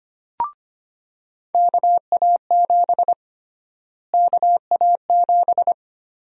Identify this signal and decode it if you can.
Morse code, heard as a keyed tone, about 25 wpm, 700 Hz. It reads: KA7 KA7
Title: USA Call Sign Prefixes - Repeated 2x - 25wpm